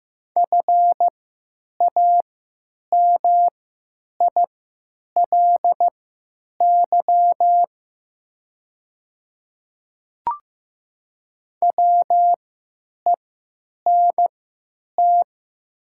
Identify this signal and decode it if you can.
Morse code, heard as a keyed tone, about 15 wpm, spaced as for 9 wpm, 700 Hz. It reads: FAMILY WENT
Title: 3x Character Spacing - Top 500 Words - 15wpm